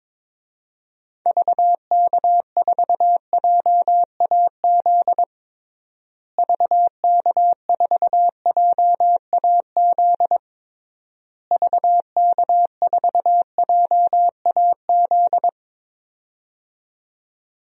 Morse code, heard as a keyed tone, about 21 wpm, 700 Hz. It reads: VK4JAZ VK4JAZ VK4JAZ
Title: DXCC-100 Call Signs - 6 Characters - Repeated 3x - 22wpm